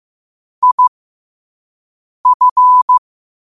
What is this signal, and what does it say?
Morse code, heard as a keyed tone, about 15 wpm, spaced as for 5 wpm, 1000 Hz. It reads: IF
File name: qotdfiles / 5wpm